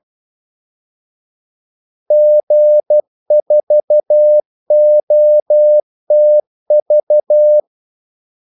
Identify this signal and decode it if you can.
Morse code, heard as a keyed tone, about 12 wpm, 600 Hz. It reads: G4OTV
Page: 2 ekstra ordmellemrum mellem kaldesignalerne